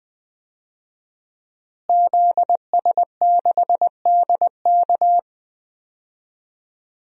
Morse code, heard as a keyed tone, about 20 wpm, 700 Hz. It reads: ZS6DK
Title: DXCC-100 Call Signs - 5 Characters - Repeated 2x - 20wpm